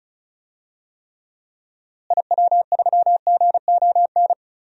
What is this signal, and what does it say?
Morse code, heard as a keyed tone, about 34 wpm, 700 Hz. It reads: IW3GON
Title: DXCC-100 Call Signs - 6 Characters - Repeated 3x - 35wpm